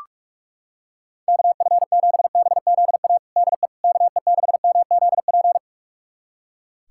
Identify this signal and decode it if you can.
Morse code, heard as a keyed tone, about 46 wpm, 700 Hz. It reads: KF7BZA DE KE6MZP
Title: USA Callsign DE Callsign - 6 Characters - 45wpm